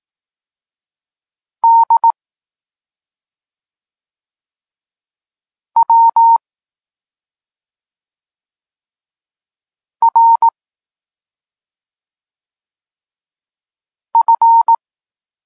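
Morse code, heard as a keyed tone, about 18 wpm, 920 Hz. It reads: D W R F
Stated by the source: Morse code lesson for learning the letters K and X, from the SP9MOA Niepołomicki Klub Krótkofalowców course.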